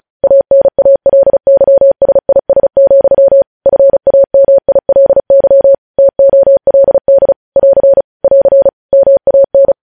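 Morse code, heard as a keyed tone, about 36 wpm, 550 Hz. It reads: ANALYSIS, FAMILY TOLD <AR> <AR> MAN
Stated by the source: News headlines sent in Morse at 35 WPM.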